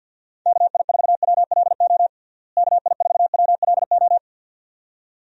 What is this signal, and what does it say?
Morse code, heard as a keyed tone, about 52 wpm, 700 Hz. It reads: KI4WLO KI4WLO
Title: USA Call Signs - 6 Characters - Repeated 2x - 50wpm